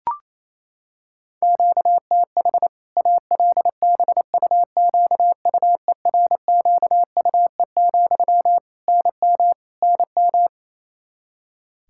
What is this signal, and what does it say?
Morse code, heard as a keyed tone, about 28 wpm, 700 Hz. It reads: QTH ALBUQUERQUE, NM NM
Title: K1USN SST QSO Practice 28wpm